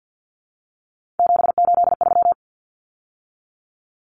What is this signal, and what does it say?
Morse code, heard as a keyed tone, about 50 wpm, 700 Hz. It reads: 783
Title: Numbers - 3-Digit 50wpm